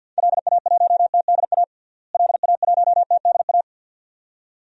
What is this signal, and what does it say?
Morse code, heard as a keyed tone, about 53 wpm, 700 Hz. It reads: LA1TDA LA1TDA
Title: DXCC-100 Call Signs - 6 Characters - Repeated 2x - 50wpm